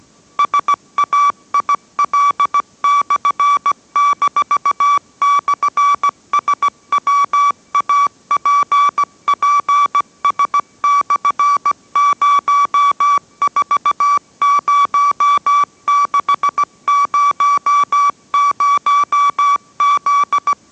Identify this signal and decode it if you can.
Morse code, heard as a keyed tone, about 17 wpm, 1150 Hz. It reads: SAIL/-/SWAPPS/040600Z